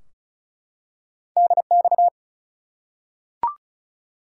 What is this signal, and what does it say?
Morse code, heard as a keyed tone, about 34 wpm, 700 Hz. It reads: DX
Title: Top 180 QSO Elements 35wpm